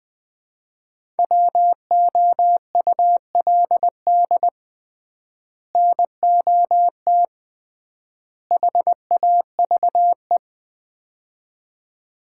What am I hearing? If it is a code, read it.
Morse code, heard as a keyed tone, about 20 wpm, 700 Hz. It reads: WOULD NOT HAVE
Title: 3x Word-Spacing - Sets of 3 Words - 20wpm